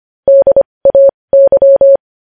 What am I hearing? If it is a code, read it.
Morse code, heard as a keyed tone, about 25 wpm, spaced as for 20 wpm, 550 Hz. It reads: DAY
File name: podcast / News headlines-1x-25x20